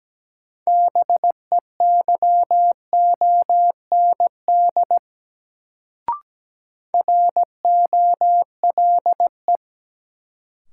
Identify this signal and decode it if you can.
Morse code, heard as a keyed tone, about 17 wpm, 700 Hz. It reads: BEYOND ROLE
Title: Sentences from Top 1000 Words - Review Part 6 - New Word Drill - 17wpm